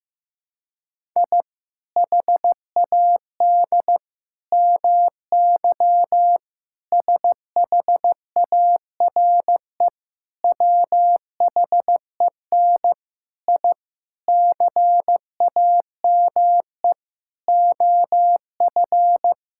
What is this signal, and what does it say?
Morse code, heard as a keyed tone, about 16 wpm, 700 Hz. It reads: I HAD MY SHARE WHEN I CAME OF
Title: Sentences from Top 1000 Words - Review Part 6 - 15wpm